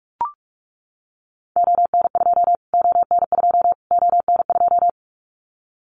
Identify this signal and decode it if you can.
Morse code, heard as a keyed tone, about 44 wpm, 700 Hz. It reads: ON2 ON2 ON2
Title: Callsign Prefixes for Top 25 DXCC - Repeated 3x - 45wpm